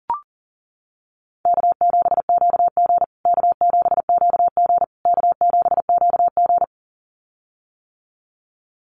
Morse code, heard as a keyed tone, about 40 wpm, 700 Hz. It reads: K7QG K7QG K7QG
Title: USA Call Signs Repeated 3x 40wpm